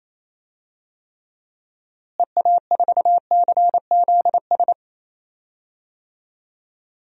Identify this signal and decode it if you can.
Morse code, heard as a keyed tone, about 28 wpm, 700 Hz. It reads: EA4CZS
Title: DXCC-100 Call Signs - 6 Characters - Repeated 3x - 28wpm